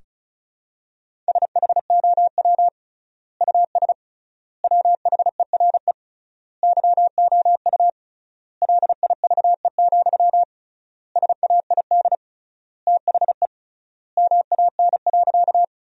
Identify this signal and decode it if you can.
Morse code, heard as a keyed tone, about 36 wpm, 700 Hz. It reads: SHOW US WHERE YOU LIVE, SAID THE MAN.
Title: 3x Word-Spacing - Sentences from Top 200 Words - 35wpm